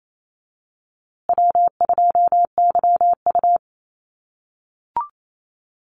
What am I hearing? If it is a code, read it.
Morse code, heard as a keyed tone, about 28 wpm, 700 Hz. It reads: W2YU
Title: USA Call Signs Repeated 3x 28wpm